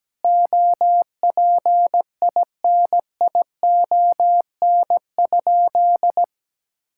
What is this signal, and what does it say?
Morse code, heard as a keyed tone, about 17 wpm, 700 Hz. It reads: OPINION?
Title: Sentences from Top 1600 Words 17wpm